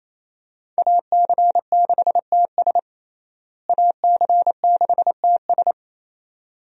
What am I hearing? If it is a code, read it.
Morse code, heard as a keyed tone, about 27 wpm, 700 Hz. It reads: AC6TS AC6TS
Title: USA Call Signs - 5 Characters - Repeated 2x - 28wpm